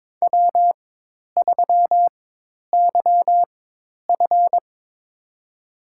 Morse code, heard as a keyed tone, about 23 wpm, spaced as for 10 wpm, 700 Hz. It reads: W3YF
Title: USA Call Signs - 4 Characters - 4x Character Spacing - 22wpm